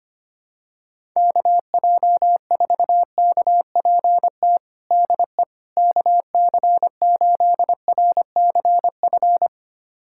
Morse code, heard as a keyed tone, about 25 wpm, 700 Hz. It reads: KJ4KPT DE KC8RCF
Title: USA Callsign DE Callsign - 6 Characters - 25wpm